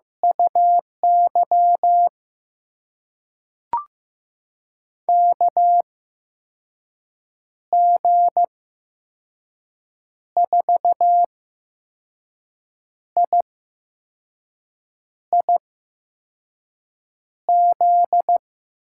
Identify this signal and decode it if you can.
Morse code, heard as a keyed tone, about 15 wpm, spaced as for 5 wpm, 700 Hz. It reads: UY KG4IIZ
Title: USA Call Signs - 6 Characters - 8x Character Spacing - 15wpm